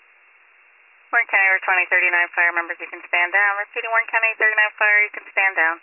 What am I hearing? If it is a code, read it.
Emergency Alert System (SAME) tones (header bursts), heard as harsh buzzing bursts, with speech. It is heard from 1.1 to 5.5 s.